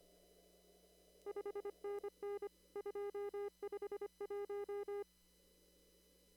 Morse code, heard as a keyed tone, about 25 wpm, 410 Hz. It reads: 5NN 251
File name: CW / kt5j wpx cw-op-k5pi